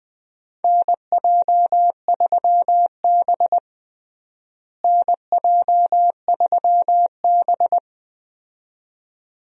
Morse code, heard as a keyed tone, about 20 wpm, 700 Hz. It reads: NJ3B NJ3B
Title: USA Call Signs - 4 Characters - Repeated 2x - 20wpm